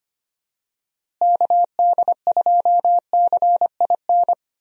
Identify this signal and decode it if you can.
Morse code, heard as a keyed tone, about 25 wpm, 700 Hz. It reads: KD2CIN
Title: USA Call Signs - 6 Characters - Repeated 3x - 25wpm